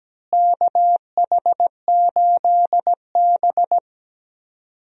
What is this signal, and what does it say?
Morse code, heard as a keyed tone, about 17 wpm, 700 Hz. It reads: KH8B
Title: USA Call Signs - 4 Characters - Repeated 2x - 17wpm